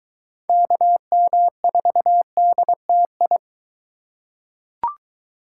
Morse code, heard as a keyed tone, about 22 wpm, 700 Hz. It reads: KM4DTI